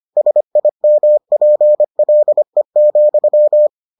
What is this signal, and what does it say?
Morse code, heard as a keyed tone, about 25 wpm, 600 Hz. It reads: SIMPLE,